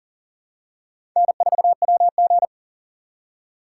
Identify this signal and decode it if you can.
Morse code, heard as a keyed tone, about 38 wpm, 700 Hz. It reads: N4WG
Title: USA Call Signs - 2x Character Spacing - 40wpm